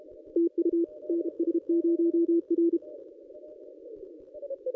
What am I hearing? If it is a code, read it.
Morse code, heard as a keyed tone, about 32 wpm, 340 Hz. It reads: TU NS0R
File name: contests / ss-cw-k5tr-op-n5rz -1400